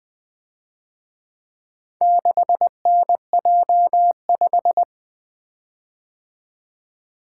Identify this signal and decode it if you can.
Morse code, heard as a keyed tone, about 23 wpm, 700 Hz. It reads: 6NJ5
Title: Speed-Racing - Four Letter-Number Combinations - 20wpm